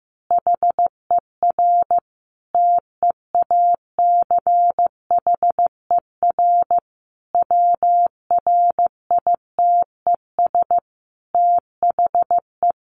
Morse code, heard as a keyed tone, about 15 wpm, 700 Hz. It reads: HER TEACHER WRITES THE
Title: Additional Easy Phrases 15wpm